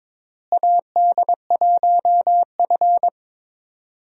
Morse code, heard as a keyed tone, about 21 wpm, 700 Hz. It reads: AD1F
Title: USA Call Signs - 4 Characters - Repeated 2x - 22wpm